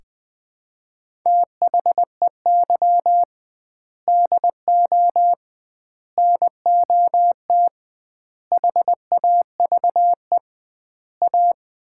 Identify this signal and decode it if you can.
Morse code, heard as a keyed tone, about 21 wpm, 700 Hz. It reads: THEY DO NOT HAVE A
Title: 2x Word-Spacing - Sets of 5 Words - 20wpm